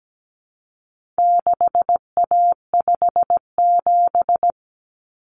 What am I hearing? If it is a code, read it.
Morse code, heard as a keyed tone, about 17 wpm, 700 Hz. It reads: 6A57
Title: Common Cut Numbers - 4-Digit 17wpm